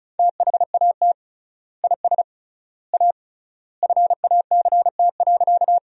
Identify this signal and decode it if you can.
Morse code, heard as a keyed tone, about 36 wpm, 700 Hz. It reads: THAT IS A FACT.
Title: 3x Word-Spacing - Sentences from Top 500 Words - 35wpm